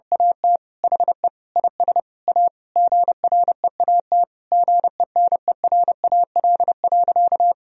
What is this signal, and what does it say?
Morse code, heard as a keyed tone, about 31 wpm, 700 Hz. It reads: AT HE IS A GREAT GENERAL.
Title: Sentences from Top 1000 Words - Review Part 3 - 30wpm